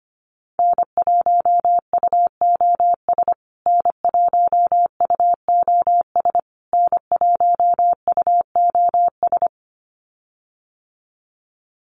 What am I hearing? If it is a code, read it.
Morse code, heard as a keyed tone, about 25 wpm, 700 Hz. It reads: N1UOS N1UOS N1UOS
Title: USA Call Signs Repeated 3x 25wpm